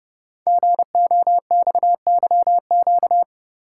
Morse code, heard as a keyed tone, about 30 wpm, 700 Hz. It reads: GOXYQ
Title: Code Groups - 5 Letter:Numbers - 30wpm